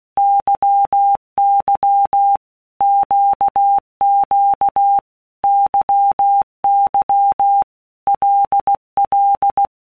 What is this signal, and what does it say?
Morse code, heard as a keyed tone, about 16 wpm, 800 Hz. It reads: YY QQ YY LL
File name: Basic Letter Groups By Blocks / Groups@15 / LFCYQ@15